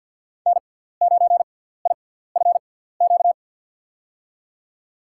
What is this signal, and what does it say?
Morse code, heard as a keyed tone, about 51 wpm, spaced as for 17 wpm, 700 Hz. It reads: N9IFQ
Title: USA Call Signs - 5 Characters - 6x Character Spacing - 50wpm